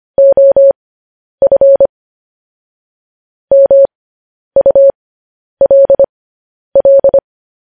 Morse code, heard as a keyed tone, about 25 wpm, spaced as for 10 wpm, 550 Hz. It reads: OF MULL
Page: news headlines in morse code at 10 wpm